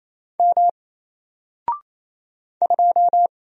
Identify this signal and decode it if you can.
Morse code, heard as a keyed tone, about 28 wpm, 700 Hz. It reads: M 2
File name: Single-Letter-Number-28wpm